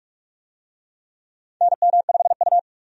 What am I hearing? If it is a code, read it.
Morse code, heard as a keyed tone, about 46 wpm, 700 Hz. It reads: NM5U
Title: USA Call Signs - 4 Characters - 45wpm